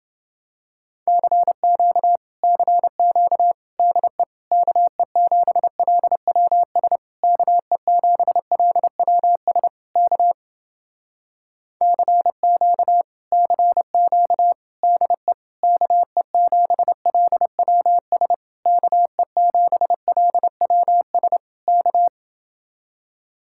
Morse code, heard as a keyed tone, about 30 wpm, 700 Hz. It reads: CQ CQ DE KE7LWS KE7LWS K CQ CQ DE KE7LWS KE7LWS K